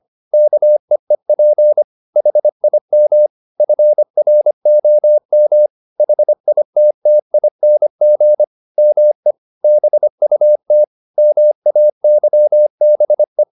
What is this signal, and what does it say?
Morse code, heard as a keyed tone, about 25 wpm, 600 Hz. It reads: KEEP HIM FROM HITTING ME BUT MAYBE